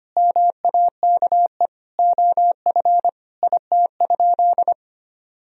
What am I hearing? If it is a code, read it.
Morse code, heard as a keyed tone, about 25 wpm, 700 Hz. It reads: MAKE OF IT?
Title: Sentences from Top 100 Words and Abbreviations 25wpm